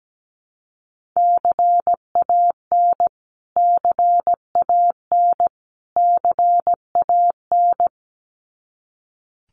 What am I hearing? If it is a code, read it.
Morse code, heard as a keyed tone, about 17 wpm, 700 Hz. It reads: CAN CAN CAN
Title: Top 300 Words Repeated 17wpm